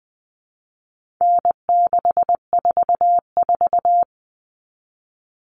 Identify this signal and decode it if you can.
Morse code, heard as a keyed tone, about 20 wpm, 700 Hz. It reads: N644
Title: All Cut Numbers - 4-Digit 20wpm